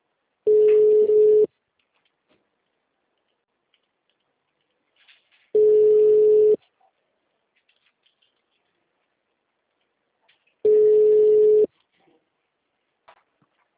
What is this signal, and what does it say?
Ringback tone (Europe, 425 Hz).
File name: out- -2016- - - .81318